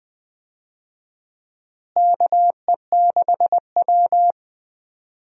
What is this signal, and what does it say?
Morse code, heard as a keyed tone, about 21 wpm, 700 Hz. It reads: KE6W
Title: USA Call Signs - 4 Characters - Repeated 2x - 20wpm